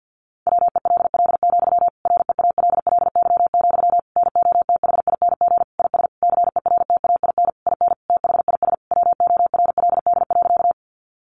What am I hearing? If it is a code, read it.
Morse code, heard as a keyed tone, about 50 wpm, 700 Hz. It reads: WELL, REALLY, NOTHING IS CERTAIN IN THIS WORLD.
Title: Sentences from Top 600 Words 50wpm